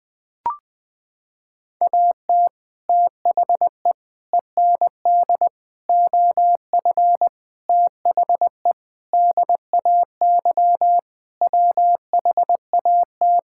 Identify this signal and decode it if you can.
Morse code, heard as a keyed tone, about 21 wpm, 700 Hz. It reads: AT THE END OF THE DAY WHAT
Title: Sentences with 6-Word Phrases - 20wpm